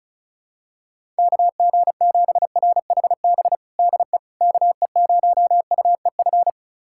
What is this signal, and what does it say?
Morse code, heard as a keyed tone, about 36 wpm, 700 Hz. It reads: KG7RHB DE KE0UEF
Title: USA Callsign DE Callsign - 6 Characters - 35wpm